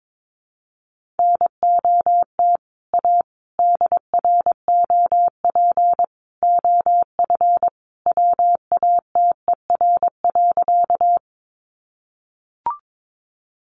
Morse code, heard as a keyed tone, about 22 wpm, 700 Hz. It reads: NOT A DROP OF WATER.
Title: Sentences from Top 1400 Words 22wpm